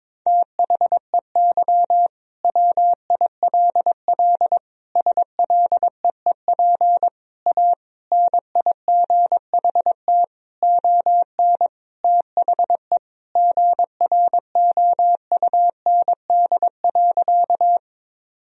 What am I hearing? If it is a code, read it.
Morse code, heard as a keyed tone, about 21 wpm, 700 Hz. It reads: THEY WILL SLEEP A NIGHT ON THE GROUND.
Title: Sentences from Top 1000 Words - Review Part 2 - 22wpm